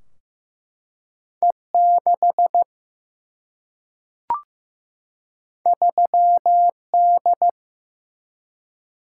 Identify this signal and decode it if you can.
Morse code, heard as a keyed tone, about 15 wpm, 700 Hz. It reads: E6 3D
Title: Single-Letter Single-Number 15wpm